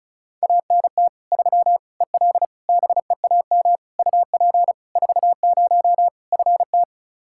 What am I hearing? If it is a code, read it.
Morse code, heard as a keyed tone, about 34 wpm, 700 Hz. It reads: ANT 3 EL BEAM UP 40 FT
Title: Practice QSO - Exchange 2 - Info - 35wpm